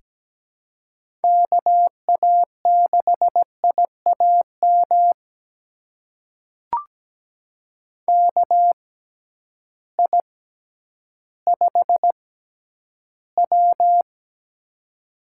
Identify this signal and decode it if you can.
Morse code, heard as a keyed tone, about 17 wpm, spaced as for 10 wpm, 700 Hz. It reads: KA6IAM KI5W
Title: USA Call Signs - 6x Character Spacing - 17wpm